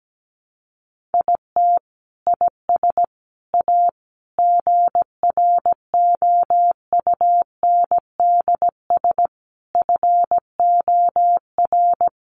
Morse code, heard as a keyed tone, about 17 wpm, 700 Hz. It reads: IT IS A GROUNDS FOR
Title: Sentences from Top 1300 Words 17wpm